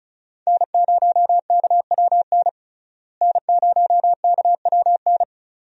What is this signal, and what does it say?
Morse code, heard as a keyed tone, about 34 wpm, 700 Hz. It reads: N0KWN N0KWN
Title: USA Call Signs - 5 Characters - Repeated 2x - 35wpm